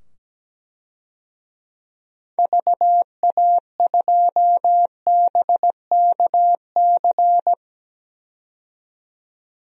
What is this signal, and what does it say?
Morse code, heard as a keyed tone, about 17 wpm, 700 Hz. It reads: VA2BKC
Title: DXCC-100 Call Signs - 6 Characters - 17wpm